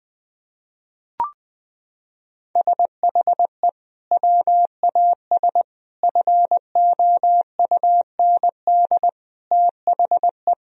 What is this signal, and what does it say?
Morse code, heard as a keyed tone, about 20 wpm, 700 Hz. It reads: SHE WAS FOUND THE
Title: Sentences from Top 1000 Words - Review Part 3 - 20wpm